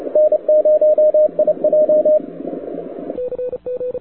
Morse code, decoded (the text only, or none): N0IJ